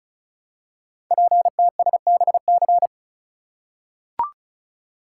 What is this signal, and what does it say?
Morse code, heard as a keyed tone, about 36 wpm, 700 Hz. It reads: PTSBC
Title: Code Groups - 5 Letters - 35wpm